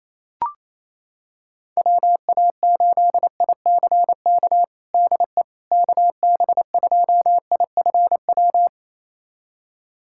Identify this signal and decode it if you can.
Morse code, heard as a keyed tone, about 28 wpm, 700 Hz. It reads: WA8ICK DE KB2IFW
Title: USA Callsign DE Callsign 28wpm